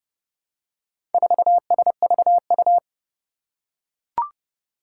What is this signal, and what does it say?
Morse code, heard as a keyed tone, about 30 wpm, 700 Hz. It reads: 4SVU
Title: Four Challenging Letter-Number Combinations 30wpm